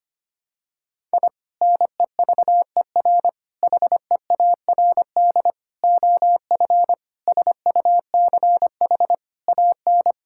Morse code, heard as a keyed tone, about 25 wpm, 700 Hz. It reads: I NEVER HEARD OF SUCH AN
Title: Sentences from Top 1000 Words - Review Part 5 - 25wpm